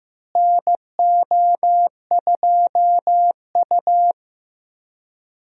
Morse code, heard as a keyed tone, about 15 wpm, 700 Hz. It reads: NO2U
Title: USA Call Signs - 4 Characters - Repeated 3x - 15wpm